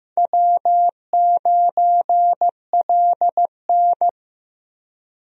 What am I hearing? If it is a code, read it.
Morse code, heard as a keyed tone, about 16 wpm, 700 Hz. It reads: W9LN
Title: USA Call Signs - 4 Characters - Repeated 3x - 15wpm